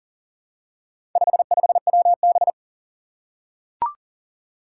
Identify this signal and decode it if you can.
Morse code, heard as a keyed tone, about 40 wpm, 700 Hz. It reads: 55WB
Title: Four Challenging Letter-Number Combinations 40wpm